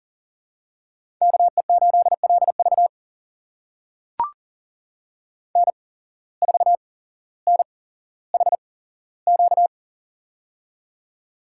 Morse code, heard as a keyed tone, about 41 wpm, spaced as for 17 wpm, 700 Hz. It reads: KE8LV N4NHQ
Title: USA Call Signs - 5 Characters - 8x Character Spacing - 40wpm